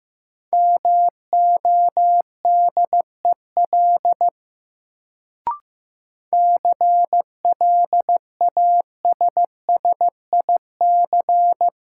Morse code, heard as a keyed tone, about 15 wpm, 700 Hz. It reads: MODEL CLASSIC
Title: Sentences from Top 1000 Words - New Word Drill - 15wpm